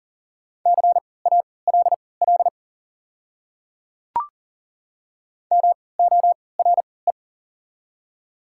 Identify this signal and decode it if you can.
Morse code, heard as a keyed tone, about 38 wpm, spaced as for 22 wpm, 700 Hz. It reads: CALL MORE
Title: 3x Character Spacing - Top 200 Words - 40wpm